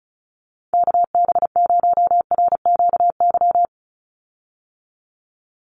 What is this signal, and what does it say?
Morse code, heard as a keyed tone, about 34 wpm, 700 Hz. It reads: KB0RQY
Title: USA Call Signs 35wpm